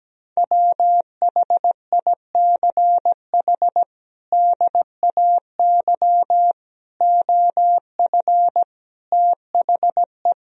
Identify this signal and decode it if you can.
Morse code, heard as a keyed tone, about 17 wpm, 700 Hz. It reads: WHICH DAY OF THE
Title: Sentences from Top 1000 Words - Review Part 3 - 17wpm